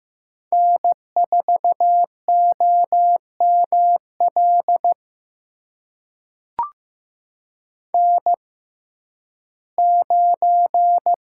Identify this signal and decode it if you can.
Morse code, heard as a keyed tone, about 15 wpm, spaced as for 11 wpm, 700 Hz. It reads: N4OML N9
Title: USA Call Signs - 5 Characters - 6x Character Spacing - 15wpm